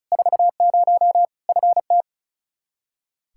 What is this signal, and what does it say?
Morse code, heard as a keyed tone, about 34 wpm, 700 Hz. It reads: 40 FT
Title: K1USN SST QSO Practice 35wpm